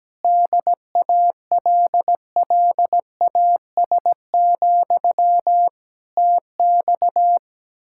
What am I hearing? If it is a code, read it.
Morse code, heard as a keyed tone, about 17 wpm, 700 Hz. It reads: DALLAS, TX
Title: Practice QSO - Exchange 1 - Beginning - 17wpm